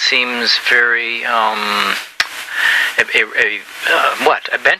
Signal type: Emergency Alert System (SAME) tones